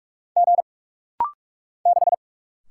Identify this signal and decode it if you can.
Morse code, heard as a keyed tone, about 47 wpm, 700 Hz. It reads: G 6
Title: Single Letter-Number - Rapid-Fire 45wpm